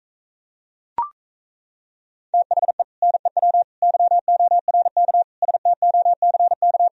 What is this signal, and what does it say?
Morse code, heard as a keyed tone, about 38 wpm, 700 Hz. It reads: THE NEW YORK STOCK